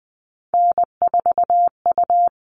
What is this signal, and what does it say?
Morse code, heard as a keyed tone, about 20 wpm, 700 Hz. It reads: N4U
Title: All Cut Numbers - 3-Digit 20wpm